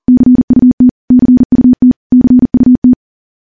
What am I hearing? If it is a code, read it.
Morse code, heard as a keyed tone, about 40 wpm, 250 Hz. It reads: CUT CUT CUT
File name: morse / morse-training-40WPM